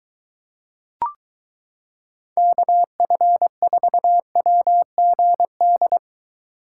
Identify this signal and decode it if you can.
Morse code, heard as a keyed tone, about 22 wpm, 700 Hz. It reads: KF4WGD